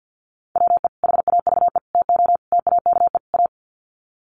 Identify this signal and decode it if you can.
Morse code, heard as a keyed tone, about 50 wpm, 700 Hz. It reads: WE HAVE TO TAKE A
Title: Sets of 5 Words Encore 50wpm